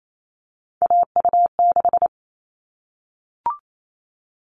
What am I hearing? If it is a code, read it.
Morse code, heard as a keyed tone, about 28 wpm, 700 Hz. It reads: AU6
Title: All Cut Numbers - 3-Digit 28wpm